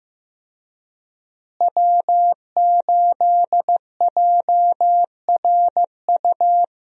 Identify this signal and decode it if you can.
Morse code, heard as a keyed tone, about 14 wpm, 700 Hz. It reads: W8JRU
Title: USA Call Signs - 5 Characters - Repeated 3x - 15wpm